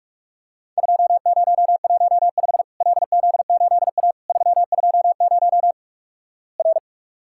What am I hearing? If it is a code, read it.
Morse code, heard as a keyed tone, about 46 wpm, 700 Hz. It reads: 2015 LZ8A 320 R
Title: Worked All Europe DX Contest - QTC Exchange - 45wpm